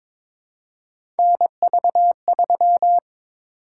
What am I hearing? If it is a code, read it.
Morse code, heard as a keyed tone, about 22 wpm, 700 Hz. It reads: NV3
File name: practice / 22wpm / Three Letter-Number Combinations - 22wpm